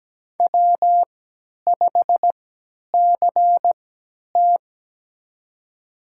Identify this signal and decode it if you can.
Morse code, heard as a keyed tone, about 17 wpm, spaced as for 10 wpm, 700 Hz. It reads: W5CT
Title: USA Call Signs - 4 Characters - 3x Character Spacing - 17wpm